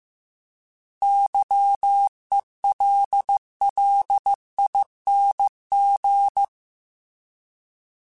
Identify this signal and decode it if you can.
Morse code, heard as a keyed tone, about 15 wpm, 800 Hz. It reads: YELLING